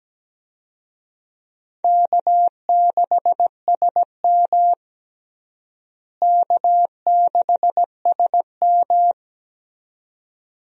Morse code, heard as a keyed tone, about 17 wpm, 700 Hz. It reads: K6SM K6SM
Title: USA Call Signs - 4 Characters - Repeated 2x - 17wpm